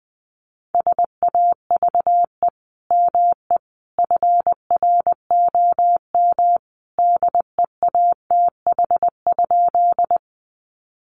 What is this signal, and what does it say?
Morse code, heard as a keyed tone, about 19 wpm, 700 Hz. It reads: SAVE ME FROM DEATH?
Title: Sentences from Top 1100 Words 20wpm